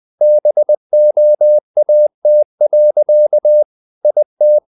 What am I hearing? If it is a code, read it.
Morse code, heard as a keyed tone, about 20 wpm, 600 Hz. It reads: BOAT. IT
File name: the old man and the sea / the old man and the sea part 1 20wpm